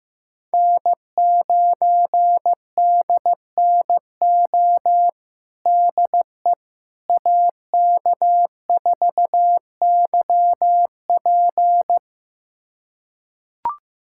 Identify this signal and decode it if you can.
Morse code, heard as a keyed tone, about 16 wpm, 700 Hz. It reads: N9DNO DE AK4YP
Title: USA Callsign DE Callsign - 5 Characters - 15wpm